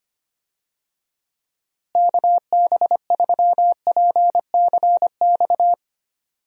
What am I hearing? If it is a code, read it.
Morse code, heard as a keyed tone, about 25 wpm, 700 Hz. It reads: KB3PCX
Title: USA Call Signs - 6 Characters - Repeated 3x - 25wpm